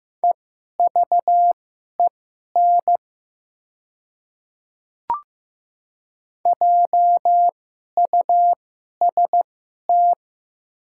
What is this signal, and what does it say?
Morse code, heard as a keyed tone, about 15 wpm, spaced as for 11 wpm, 700 Hz. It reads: EVEN JUST
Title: 2x Character Spacing - Top 200 Words - 15wpm